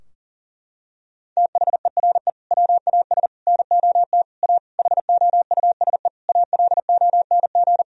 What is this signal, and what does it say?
Morse code, heard as a keyed tone, about 40 wpm, 700 Hz. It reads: THERE WAS NOT A HOUSE ALONG
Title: Sentences from Top 1000 Words - Review Part 2 - 40wpm